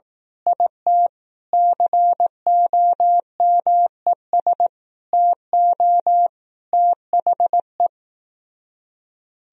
Morse code, heard as a keyed tone, about 18 wpm, 700 Hz. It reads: IT COMES TO THE
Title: Speed-Racing - Sets of 5 Words 15wpm